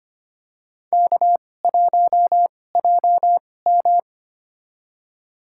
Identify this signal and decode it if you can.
Morse code, heard as a keyed tone, about 25 wpm, spaced as for 18 wpm, 700 Hz. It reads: K1JM
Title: USA Call Signs - 4 Characters - 2x Character Spacing - 25wpm